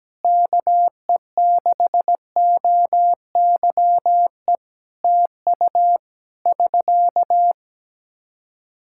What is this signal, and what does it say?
Morse code, heard as a keyed tone, about 17 wpm, 700 Hz. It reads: KE6OYE TU <SK>
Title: Practice QSO - Exchange 4 - Ending - 17wpm